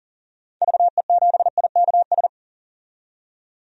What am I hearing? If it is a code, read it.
Morse code, heard as a keyed tone, about 40 wpm, 700 Hz. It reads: VE7IKS
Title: DXCC-100 Call Signs - 6 Characters - 40wpm